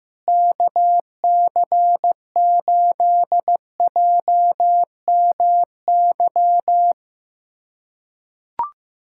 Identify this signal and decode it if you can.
Morse code, heard as a keyed tone, about 15 wpm, 700 Hz. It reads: KC8JMY
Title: USA Callsign DE Callsign - 6 Characters - 15wpm